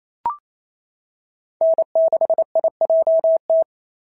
Morse code, heard as a keyed tone, about 28 wpm, 690 Hz. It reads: N6IJT
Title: SOTA QSO Practice 28wpm